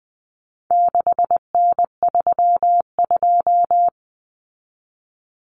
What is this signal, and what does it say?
Morse code, heard as a keyed tone, about 20 wpm, 700 Hz. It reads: 6N32
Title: Common Cut Numbers - 4-Digit 20wpm